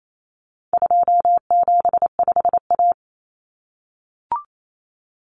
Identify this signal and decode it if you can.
Morse code, heard as a keyed tone, about 28 wpm, 700 Hz. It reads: 275A
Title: Common Cut Numbers - 4-Digit 28wpm